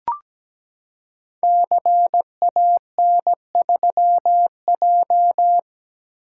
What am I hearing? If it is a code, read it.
Morse code, heard as a keyed tone, about 17 wpm, 700 Hz. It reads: CAN3J
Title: Code Groups - 5 Letter:Numbers - 17wpm